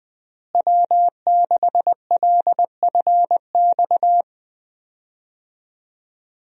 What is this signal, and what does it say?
Morse code, heard as a keyed tone, about 20 wpm, 700 Hz. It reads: W6LFX
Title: USA Call Signs - 5 Characters - Repeated 2x - 20wpm